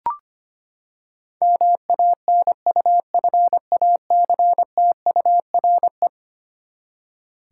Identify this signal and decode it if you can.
Morse code, heard as a keyed tone, about 25 wpm, 700 Hz. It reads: MANUFACTURE
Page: morse-code-ninja-practice-25wpm